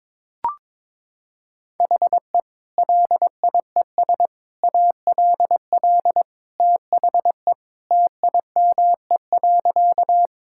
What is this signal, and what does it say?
Morse code, heard as a keyed tone, about 22 wpm, 700 Hz. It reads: HE LIES ALL THE TIME.
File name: practice / 22wpm / Sentences with 3-Word Phrases - 22wpm